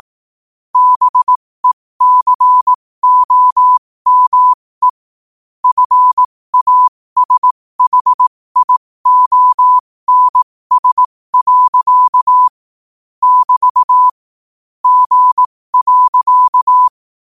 Morse code, heard as a keyed tone, about 18 wpm, spaced as for 15 wpm, 1000 Hz. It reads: BECOME FASHIONS. <BT> G.
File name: qotdfiles / 15wpm